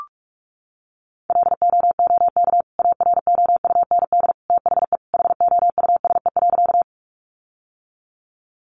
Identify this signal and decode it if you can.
Morse code, heard as a keyed tone, about 44 wpm, 700 Hz. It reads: LOOK AROUND THE HOUSE.
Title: Sentences from Top 600 Words 45wpm